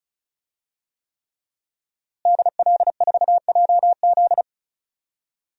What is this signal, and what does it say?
Morse code, heard as a keyed tone, about 34 wpm, 700 Hz. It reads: DL4JZ
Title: DXCC-100 Call Signs - 5 Characters - Repeated 3x - 35wpm